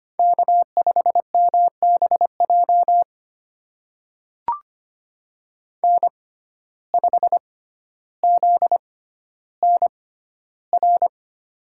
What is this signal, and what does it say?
Morse code, heard as a keyed tone, about 25 wpm, spaced as for 13 wpm, 700 Hz. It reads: K5MBJ N5ZNR
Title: USA Call Signs - 5 Characters - 6x Character Spacing - 25wpm